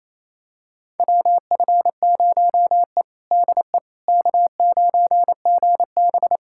Morse code, heard as a keyed tone, about 28 wpm, 700 Hz. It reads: WF0E DE K9GB
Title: USA Callsign DE Callsign - 4 Characters - 28wpm